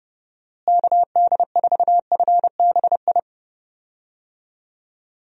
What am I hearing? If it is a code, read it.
Morse code, heard as a keyed tone, about 30 wpm, 700 Hz. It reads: KD4FBI
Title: USA Call Signs - 6 Characters - Repeated 3x - 30wpm